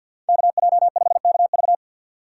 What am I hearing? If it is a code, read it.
Morse code, heard as a keyed tone, about 61 wpm, 700 Hz. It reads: KJ5KV
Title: Speed Racing - USA 2x2 Call Signs 50wpm